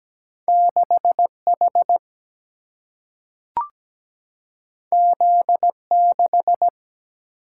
Morse code, heard as a keyed tone, about 17 wpm, 700 Hz. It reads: 6 H Z 6
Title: Single-Letter Single-Number 17wpm